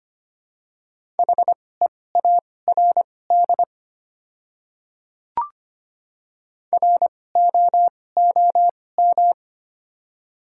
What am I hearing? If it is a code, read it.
Morse code, heard as a keyed tone, about 25 wpm, spaced as for 18 wpm, 700 Hz. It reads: HEARD ROOM
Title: 2x Character Spacing - Top 500 Words - 25wpm